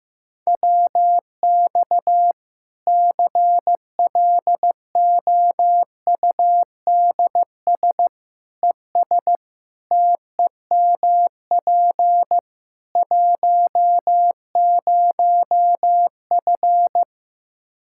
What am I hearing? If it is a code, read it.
Morse code, heard as a keyed tone, about 15 wpm, 700 Hz. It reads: WX CLOUDS ES TEMP 10F
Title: Practice QSO - Exchange 2 - Info - 15wpm